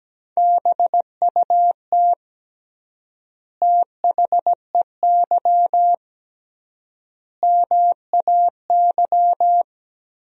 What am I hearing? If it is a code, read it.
Morse code, heard as a keyed tone, about 17 wpm, 700 Hz. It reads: BUT THEY MAY
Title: 3x Word-Spacing - Sentences from Top 300 Words - 17wpm